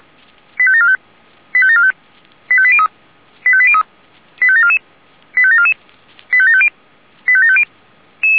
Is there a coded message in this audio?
Yes: five-tone selective call (ZVEI) from 0.6 to 7.6 s, heard as quick runs of five tones, tones of about 70 ms. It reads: 85745 85745 85803 85803 85744 85744 85750 85750